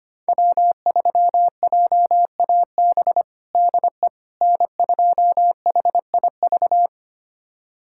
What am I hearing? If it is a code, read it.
Morse code, heard as a keyed tone, about 25 wpm, 700 Hz. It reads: W3JAB DE N2HIV
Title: USA Callsign DE Callsign - 5 Characters - 25wpm